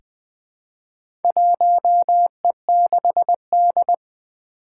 Morse code, heard as a keyed tone, about 20 wpm, 700 Hz. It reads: 1E6D
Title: Four Challenging Letter-Number Combinations 20wpm